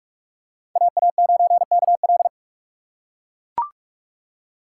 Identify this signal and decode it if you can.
Morse code, heard as a keyed tone, about 41 wpm, 700 Hz. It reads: AA9KL
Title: USA Call Signs - 5 Characters - 4x Character Spacing - 45wpm